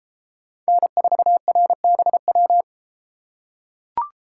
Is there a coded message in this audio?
Yes: Morse code from 0.7 to 2.6 s, heard as a keyed tone, about 31 wpm, 700 Hz. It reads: N4RBW